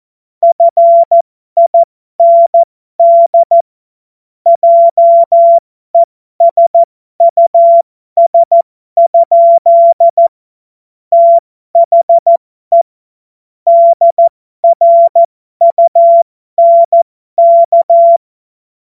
Morse code, heard as a keyed tone, about 14 wpm, spaced as for 12 wpm, 680 Hz. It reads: FIND JESUS? THE DRUNK
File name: Morsekurs / Witze